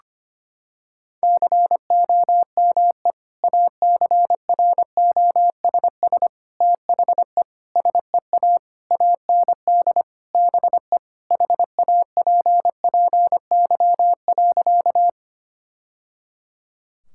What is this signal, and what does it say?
Morse code, heard as a keyed tone, about 25 wpm, 700 Hz. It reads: COME ACROSS THE SEA AND BE HAPPY.
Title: Sentences from Top 1000 Words - Review Part 5 - 25wpm